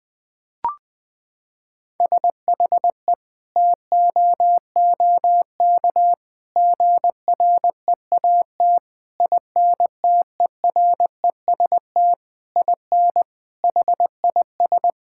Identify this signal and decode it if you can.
Morse code, heard as a keyed tone, about 20 wpm, 700 Hz. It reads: SHE TOOK GREAT INTEREST IN HIS
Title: Sentences from Top 1000 Words - Review Part 6 - 20wpm